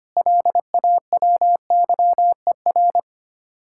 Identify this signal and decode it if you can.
Morse code, heard as a keyed tone, about 31 wpm, 700 Hz. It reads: LAWYER
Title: Speed-Racing Common QSO Phrases Encore - 25wpm